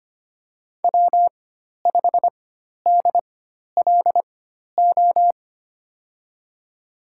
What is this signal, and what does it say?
Morse code, heard as a keyed tone, about 25 wpm, spaced as for 12 wpm, 700 Hz. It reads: W5DLO
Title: USA Call Signs - 5 Characters - 4x Character Spacing - 25wpm